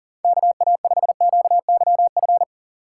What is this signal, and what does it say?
Morse code, heard as a keyed tone, about 40 wpm, 700 Hz. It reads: KA5QYF
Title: USA Call Signs - 6 Characters - 3x Character Spacing - 40wpm